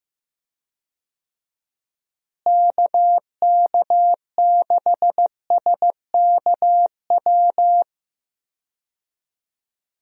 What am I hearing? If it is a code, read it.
Morse code, heard as a keyed tone, about 15 wpm, 700 Hz. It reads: KK6SKW
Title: USA Call Signs - 6 Characters - Repeated 3x - 15wpm